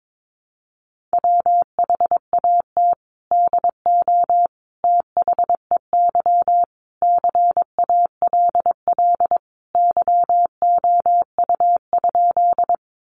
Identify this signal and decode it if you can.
Morse code, heard as a keyed tone, about 22 wpm, 700 Hz. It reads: WHAT DO THEY CALL YOU?
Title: Sentences from Top 100 Words 22wpm